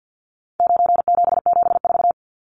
Morse code, heard as a keyed tone, about 50 wpm, 700 Hz. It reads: 9774
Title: Numbers - 4-Digit 50wpm